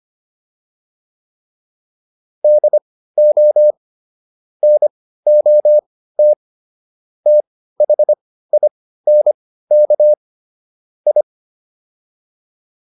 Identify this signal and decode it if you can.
Morse code, heard as a keyed tone, about 25 wpm, spaced as for 15 wpm, 600 Hz. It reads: DO NOT THINK I